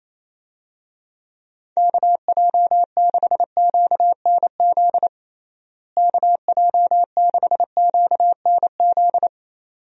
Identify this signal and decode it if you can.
Morse code, heard as a keyed tone, about 28 wpm, 700 Hz. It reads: KJ6QNZ KJ6QNZ
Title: USA Call Signs - 6 Characters - Repeated 2x - 28wpm